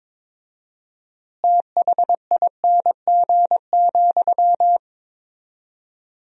Morse code, heard as a keyed tone, about 22 wpm, 700 Hz. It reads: THING,
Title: 4x Word-Spacing - Sentences from Top 200 Words - 22wpm